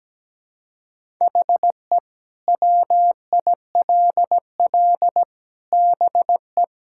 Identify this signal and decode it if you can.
Morse code, heard as a keyed tone, about 17 wpm, 700 Hz. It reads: HE WILL BE
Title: Sentences from Top 1000 Words - Review Part 5 - 17wpm